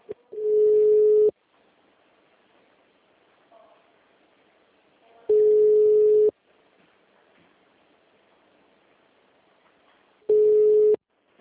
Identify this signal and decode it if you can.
Ringback tone (Europe, 425 Hz).